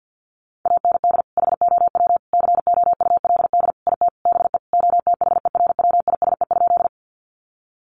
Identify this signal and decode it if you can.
Morse code, heard as a keyed tone, about 48 wpm, 700 Hz. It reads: AND HOW COULD IT BE OTHERWISE?
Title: Sentences from Top 1200 Words 50wpm